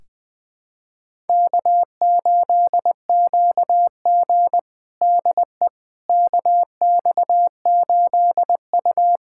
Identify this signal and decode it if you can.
Morse code, heard as a keyed tone, about 20 wpm, 700 Hz. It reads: K8QG DE KX8U
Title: USA Callsign DE Callsign - 4 Characters - 20wpm